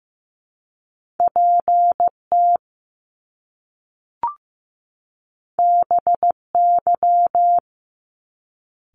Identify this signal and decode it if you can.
Morse code, heard as a keyed tone, about 15 wpm, 700 Hz. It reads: PT BY
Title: Top Two Letter Combinations in Words 15wpm